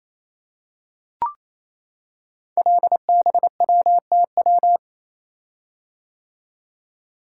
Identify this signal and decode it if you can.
Morse code, heard as a keyed tone, about 28 wpm, 690 Hz. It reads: LBWTW
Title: Code Groups - 5 Letter:Numbers - 28wpm